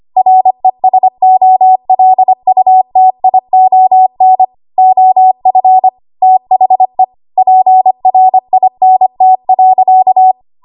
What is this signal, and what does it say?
Morse code, heard as a keyed tone, about 25 wpm, 750 Hz. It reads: RESOLUTION OF THE PRINT.
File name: Archive / 25 WPM / 25WPM